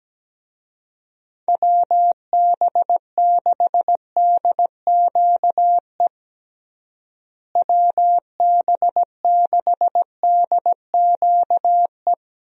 Morse code, heard as a keyed tone, about 17 wpm, 700 Hz. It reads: WB6DQE WB6DQE
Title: USA Call Signs - 6 Characters - Repeated 2x - 17wpm